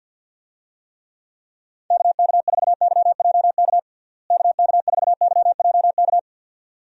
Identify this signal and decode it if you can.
Morse code, heard as a keyed tone, about 50 wpm, 700 Hz. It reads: KK4YJK KK4YJK
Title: USA Call Signs - 6 Characters - Repeated 2x - 50wpm